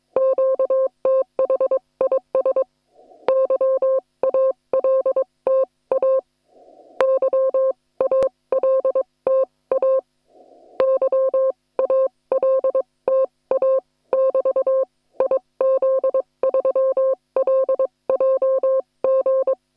Morse code, heard as a keyed tone, about 21 wpm, spaced as for 18 wpm, 530 Hz. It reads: QTH IS YALTA YALTA YALTA <BT> IZ3LJG